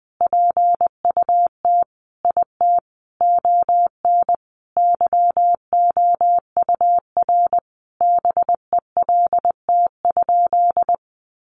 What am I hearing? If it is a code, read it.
Morse code, heard as a keyed tone, about 19 wpm, 700 Hz. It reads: PUT IT ON YOUR BELT?
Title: Sentences from Top 1400 Words 20wpm